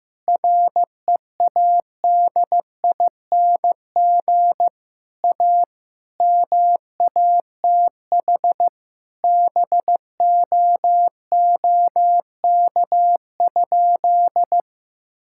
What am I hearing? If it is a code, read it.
Morse code, heard as a keyed tone, about 15 wpm, 700 Hz. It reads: READING A MATH BOOK?
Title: Sentences from Top 1000 Words - Review Part 6 - 15wpm